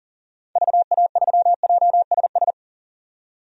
Morse code, heard as a keyed tone, about 37 wpm, 700 Hz. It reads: VA3JSS
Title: DXCC-100 Call Signs - 6 Characters - 40wpm